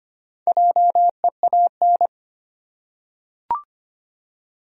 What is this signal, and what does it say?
Morse code, heard as a keyed tone, about 25 wpm, 700 Hz. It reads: JEAN
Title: USA Common Names - Max 5 Letters - 25wpm